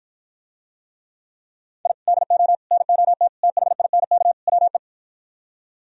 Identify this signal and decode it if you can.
Morse code, heard as a keyed tone, about 52 wpm, 700 Hz. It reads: I DO NOT THINK WE